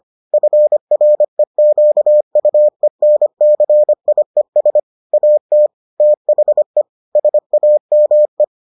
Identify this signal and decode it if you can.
Morse code, heard as a keyed tone, about 25 wpm, 600 Hz. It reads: FREQUENCIES AT THE SAME